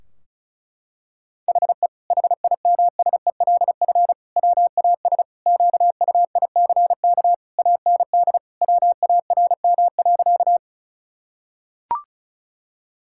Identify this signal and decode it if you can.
Morse code, heard as a keyed tone, about 36 wpm, 700 Hz. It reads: HE HIMSELF WAS QUICK AND WARM.
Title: Sentences from Top 1600 Words 35wpm